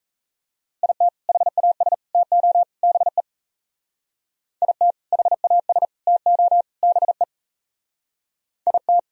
Morse code, heard as a keyed tone, about 34 wpm, 700 Hz. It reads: IT HAS TO BE IT HAS TO BE IT
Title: Speed-Racing - Sets of 4 Words 28wpm